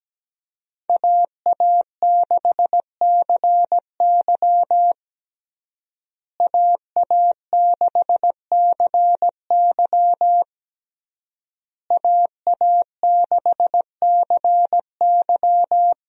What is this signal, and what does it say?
Morse code, heard as a keyed tone, about 17 wpm, 700 Hz. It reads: AA6CY AA6CY AA6CY
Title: USA Call Signs - 5 Characters - Repeated 3x - 17wpm